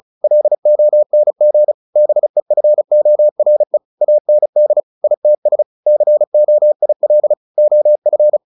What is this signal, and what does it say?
Morse code, heard as a keyed tone, about 34 wpm, 600 Hz. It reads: LONG BEFORE AND ITS COIL OF
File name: the old man and the sea 2 35wpm